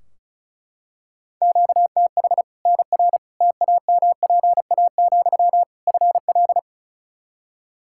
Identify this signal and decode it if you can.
Morse code, heard as a keyed tone, about 36 wpm, 700 Hz. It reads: QTH NR TAMPA, FL
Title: Practice QSO - Exchange 1 - Beginning - 35wpm